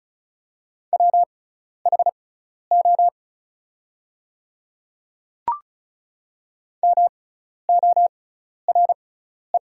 Morse code, heard as a keyed tone, about 36 wpm, spaced as for 12 wpm, 700 Hz. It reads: WHO MORE
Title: 6x Character Spacing - Top 100 Words - 35wpm